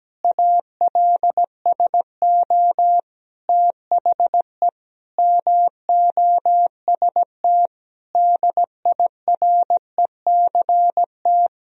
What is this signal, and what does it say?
Morse code, heard as a keyed tone, about 17 wpm, 700 Hz. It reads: ALSO THE MOST DIRECT
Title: Sentences from Top 1000 Words - Review Part 2 - 17wpm